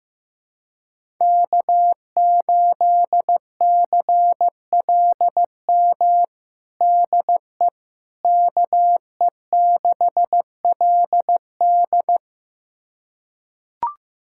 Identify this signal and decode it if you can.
Morse code, heard as a keyed tone, about 15 wpm, 700 Hz. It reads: K8CLM DE KE6LD
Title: USA Callsign DE Callsign - 5 Characters - 15wpm